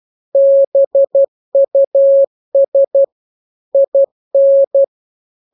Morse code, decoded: BUS IN